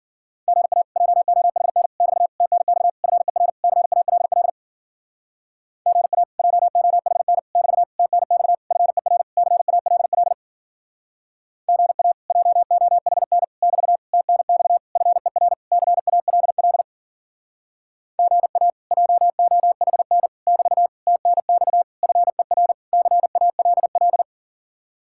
Morse code, heard as a keyed tone, about 53 wpm, 700 Hz. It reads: GA JOHN <BT> TNX FER CALL GA JOHN <BT> TNX FER CALL GA JOHN <BT> TNX FER CALL GA JOHN <BT> TNX FER CALL